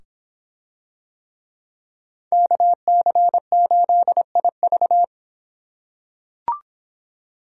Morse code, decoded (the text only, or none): KC8IV